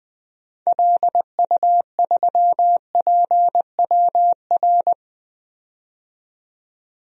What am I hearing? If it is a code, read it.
Morse code, heard as a keyed tone, about 20 wpm, 700 Hz. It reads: LU3PWR
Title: DXCC-100 Call Signs - 6 Characters - 20wpm